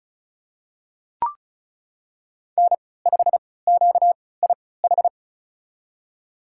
Morse code, heard as a keyed tone, about 36 wpm, spaced as for 20 wpm, 700 Hz. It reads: N5QIH
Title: USA Call Signs - 5 Characters - 3x Character Spacing - 35wpm